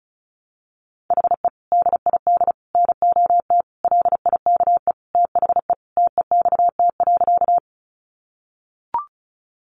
Morse code, heard as a keyed tone, about 34 wpm, 700 Hz. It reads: HE DID NOT LIKE THE TEXT.
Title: Sentences from Top 1200 Words 35wpm